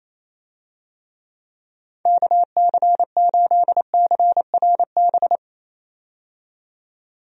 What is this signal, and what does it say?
Morse code, heard as a keyed tone, about 28 wpm, 700 Hz. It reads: KC8CRB
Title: USA Call Signs - 6 Characters - 28wpm